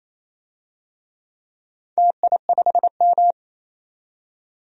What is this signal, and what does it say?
Morse code, heard as a keyed tone, about 27 wpm, 700 Hz. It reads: TI5M
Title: DXCC-100 Call Signs - 4 Characters - 28wpm